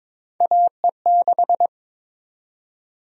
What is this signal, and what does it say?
Morse code, heard as a keyed tone, about 22 wpm, 700 Hz. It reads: AE6
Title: USA Call Sign Prefixes - 22wpm